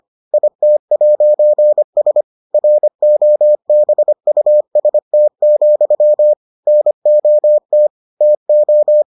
Morse code, heard as a keyed tone, about 25 wpm, 600 Hz. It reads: IT'S ROBUST, NOT TO